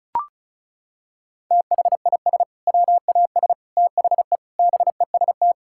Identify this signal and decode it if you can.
Morse code, heard as a keyed tone, about 36 wpm, 700 Hz. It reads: THIS WAS THE BEST
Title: Sentences from Top 1600 Words 35wpm